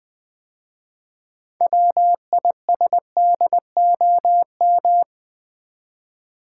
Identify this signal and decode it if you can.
Morse code, heard as a keyed tone, about 20 wpm, 700 Hz. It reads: WISDOM
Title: 6 Letter Words - First 3 Letters Given - 20wpm